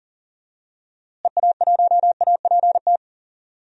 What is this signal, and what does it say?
Morse code, heard as a keyed tone, about 39 wpm, 700 Hz. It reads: EA1APT
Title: DXCC-100 Call Signs - 6 Characters - Repeated 3x - 40wpm